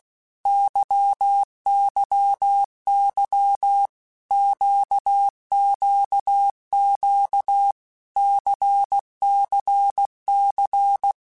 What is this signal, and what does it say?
Morse code, heard as a keyed tone, about 16 wpm, 800 Hz. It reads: YYY QQQ CCC
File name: Basic Letter Groups By7Groups / LFCYQ / LFCYQ@15